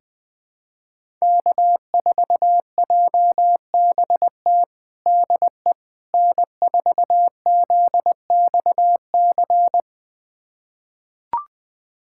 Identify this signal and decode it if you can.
Morse code, heard as a keyed tone, about 21 wpm, 700 Hz. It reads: K4JBT DE N4ZXC
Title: USA Callsign DE Callsign - 5 Characters - 20wpm